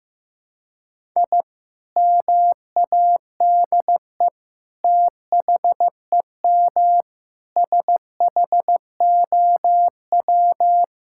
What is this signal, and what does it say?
Morse code, heard as a keyed tone, about 15 wpm, 700 Hz. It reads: I MADE THEM SHOW
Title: Sentences from Top 1000 Words - Review Part 6 - 15wpm